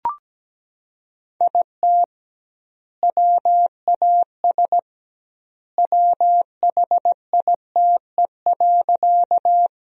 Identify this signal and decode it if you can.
Morse code, heard as a keyed tone, about 17 wpm, 700 Hz. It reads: IT WAS WHITE.
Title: 2x Word-Spacing - Sentences from Top 300 Words - 17wpm